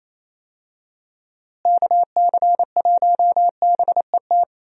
Morse code, heard as a keyed tone, about 28 wpm, 700 Hz. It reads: KC1BET
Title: USA Call Signs - 6 Characters - Repeated 2x - 28wpm